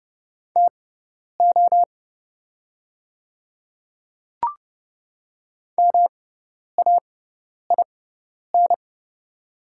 Morse code, heard as a keyed tone, about 32 wpm, spaced as for 10 wpm, 700 Hz. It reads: TO MAIN
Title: 6x Character Spacing - Top 300 Words - 30wpm